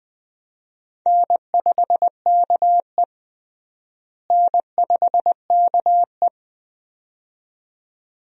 Morse code, heard as a keyed tone, about 21 wpm, 700 Hz. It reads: N5KE N5KE
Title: USA Call Signs - 4 Characters - Repeated 2x - 20wpm